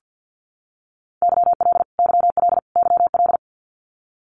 Morse code, heard as a keyed tone, about 50 wpm, 700 Hz. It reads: YL YL YL
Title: Abbreviations Repeated 3x - 50wpm